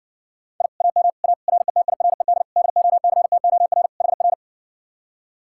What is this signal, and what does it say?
Morse code, heard as a keyed tone, about 60 wpm, 700 Hz. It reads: I AM A RETIRED DOCTOR HR